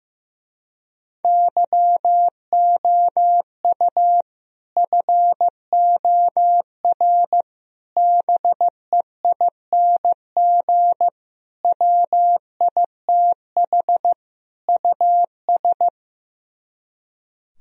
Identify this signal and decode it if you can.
Morse code, heard as a keyed tone, about 15 wpm, 700 Hz. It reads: YOU FOR BEING WITH US
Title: Speed-Racing - Sets of 5 Words 15wpm